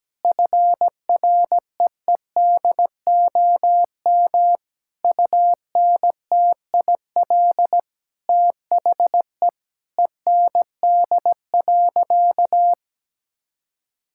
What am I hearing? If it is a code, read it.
Morse code, heard as a keyed tone, about 17 wpm, 700 Hz. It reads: FREEDOM UNTIL THE END.
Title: Sentences from Top 1000 Words - Review Part 5 - 17wpm